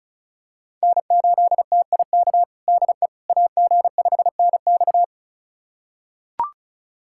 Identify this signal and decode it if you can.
Morse code, heard as a keyed tone, about 36 wpm, 700 Hz. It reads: N8TIK DE AG5NX
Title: USA Callsign DE Callsign - 5 Characters - 35wpm